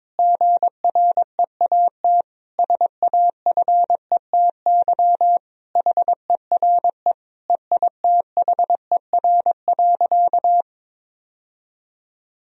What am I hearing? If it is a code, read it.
Morse code, heard as a keyed tone, about 22 wpm, 700 Hz. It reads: GREAT SAFETY HERE EITHER.
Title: Sentences from Top 1000 Words - Review Part 4 - 22wpm